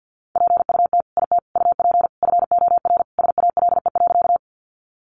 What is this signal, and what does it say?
Morse code, heard as a keyed tone, about 50 wpm, 700 Hz. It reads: PUT IT UP FOR SALE.
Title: Sentences from Top 1400 Words 50wpm